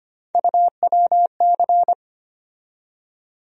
Morse code, heard as a keyed tone, about 25 wpm, 700 Hz. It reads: UWC
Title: USA Call Sign Postfixes - 3 Letters 25wpm